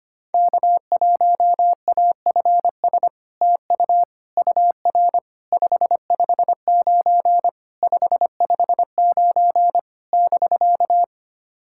Morse code, heard as a keyed tone, about 25 wpm, 690 Hz. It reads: K1AFS TU UR 559 559 <BK>
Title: SOTA QSO Practice 25wpm